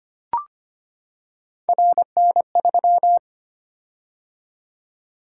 Morse code, heard as a keyed tone, about 25 wpm, 700 Hz. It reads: RN3
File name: Three Letter-Number Combinations - 25wpm